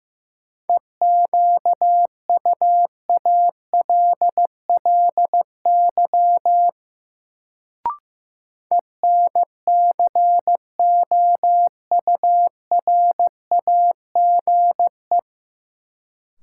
Morse code, heard as a keyed tone, about 15 wpm, 700 Hz. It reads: EQUALLY ENCOURAGE
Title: Sentences from Top 1400 Words - New Word Drill - 15wpm